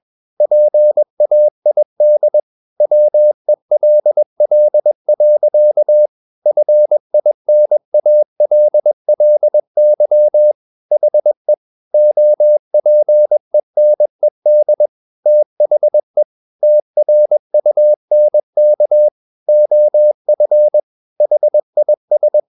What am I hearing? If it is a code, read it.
Morse code, heard as a keyed tone, about 21 wpm, 600 Hz. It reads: PAID WELL. FINALLY HE OPENED THE TRUNK OF HIS
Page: CW med Gnister